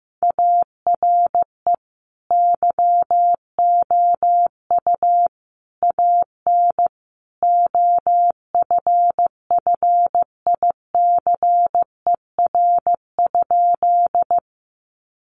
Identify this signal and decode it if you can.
Morse code, heard as a keyed tone, about 15 wpm, 700 Hz. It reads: ARE YOU AN OFFICER?
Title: Sentences from Top 1200 Words 15wpm